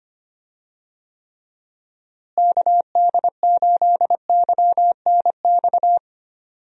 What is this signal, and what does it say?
Morse code, heard as a keyed tone, about 25 wpm, 700 Hz. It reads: KD8YNX
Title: USA Call Signs - 6 Characters - Repeated 2x - 25wpm